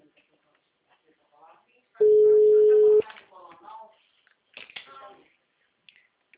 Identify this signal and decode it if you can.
Ringback tone (Europe, 425 Hz), with speech.
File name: out- -2014- - - .52896